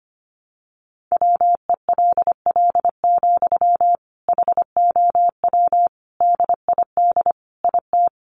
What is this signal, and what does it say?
Morse code, heard as a keyed tone, about 25 wpm, 700 Hz. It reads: WELL, HOW DID IT
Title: Sentences from Top 200 Words 25wpm